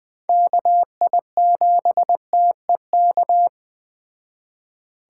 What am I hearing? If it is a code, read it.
Morse code, heard as a keyed tone, about 21 wpm, 700 Hz. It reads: KI7TEK